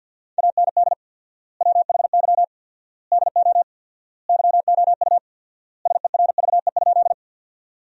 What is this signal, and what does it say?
Morse code, heard as a keyed tone, about 53 wpm, 700 Hz. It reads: AND WHY DO YOU SERVE?
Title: 4x Word-Spacing - Sentences from Top 400 Words - 50wpm